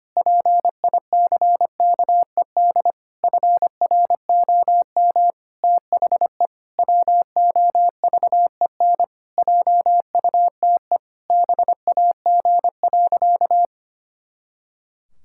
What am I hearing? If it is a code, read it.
Morse code, heard as a keyed tone, about 25 wpm, 700 Hz. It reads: PICKED FROM THE WOVEN JUTE BAG.
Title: Pangram Challenge v2 - 25wpm